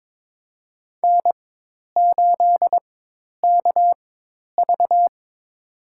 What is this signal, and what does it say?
Morse code, heard as a keyed tone, about 21 wpm, spaced as for 10 wpm, 700 Hz. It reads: N8KV
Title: USA Call Signs - 4 Characters - 4x Character Spacing - 22wpm